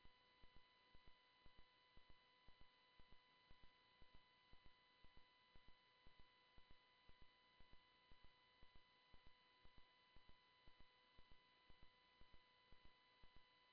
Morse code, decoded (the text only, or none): none